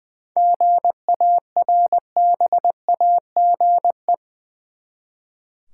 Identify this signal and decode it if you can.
Morse code, heard as a keyed tone, about 20 wpm, 700 Hz. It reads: GARBAGE
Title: Seven Letter Words - 20wpm